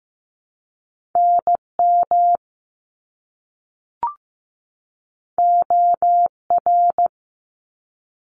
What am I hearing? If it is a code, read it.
Morse code, heard as a keyed tone, about 15 wpm, 700 Hz. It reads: NM OR
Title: ARRL Sections 15wpm